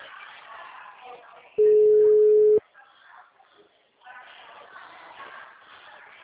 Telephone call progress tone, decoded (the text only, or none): ringback tone